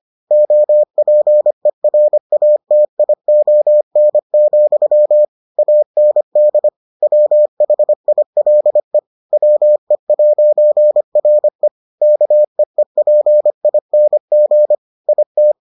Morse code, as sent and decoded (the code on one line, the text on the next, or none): --- .--. .-. .- - .. --- -. --..-- / .- -. -.. / .-- .... .. .-.. / .-- . .----. .-. / -.- . .--. .. -. --. / .. -
OPERATION, AND WHILE WE'RE KEEPING IT